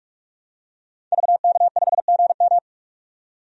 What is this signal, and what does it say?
Morse code, heard as a keyed tone, about 46 wpm, 700 Hz. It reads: VK5GM
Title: DXCC-100 Call Signs - 5 Characters - 45wpm